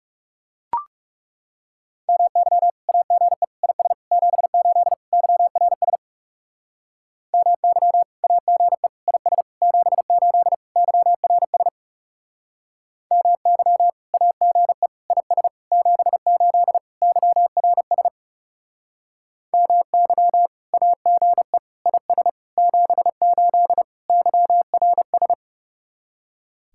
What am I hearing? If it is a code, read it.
Morse code, heard as a keyed tone, about 40 wpm, 700 Hz. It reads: MY AGE IS 78 YRS MY AGE IS 78 YRS MY AGE IS 78 YRS MY AGE IS 78 YRS